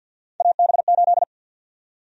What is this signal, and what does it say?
Morse code, heard as a keyed tone, about 53 wpm, 700 Hz. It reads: AB8
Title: USA Call Sign Prefixes - Repeated 2x - 50wpm